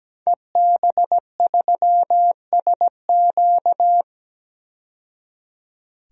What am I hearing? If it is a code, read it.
Morse code, heard as a keyed tone, about 17 wpm, 700 Hz. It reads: EB3SQ
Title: DXCC-100 Call Signs - 5 Characters - 17wpm